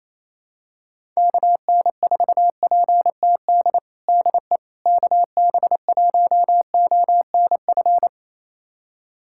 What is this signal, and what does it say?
Morse code, heard as a keyed tone, about 27 wpm, 700 Hz. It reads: KN4PTD DE KB1ONF
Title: USA Callsign DE Callsign - 6 Characters - 28wpm